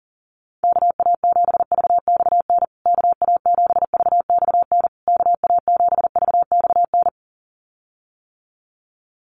Morse code, heard as a keyed tone, about 37 wpm, 700 Hz. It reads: KA7VXN KA7VXN KA7VXN
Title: USA Call Signs Repeated 3x 40wpm